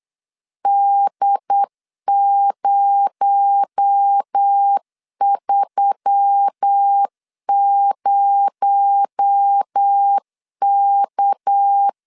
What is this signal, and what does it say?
Morse code, heard as a keyed tone, about 8 wpm, 800 Hz. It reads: D030K
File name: LEZIONI / 6 PKX0 / 6 PKX0 8wpm 3341